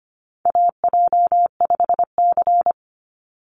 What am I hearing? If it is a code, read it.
Morse code, heard as a keyed tone, about 25 wpm, 700 Hz. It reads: AJ5C
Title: NA SOTA Chaser Callsigns 25wpm